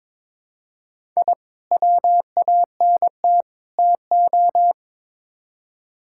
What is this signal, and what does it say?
Morse code, heard as a keyed tone, about 21 wpm, 700 Hz. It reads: I WANT TO
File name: practice / 22wpm / Build A Phrase - 5-Word Phrases - 22wpm